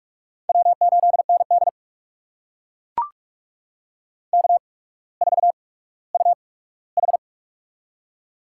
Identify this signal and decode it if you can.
Morse code, heard as a keyed tone, about 44 wpm, spaced as for 19 wpm, 700 Hz. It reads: W8ND K4UH
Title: USA Call Signs - 4 Characters - 8x Character Spacing - 45wpm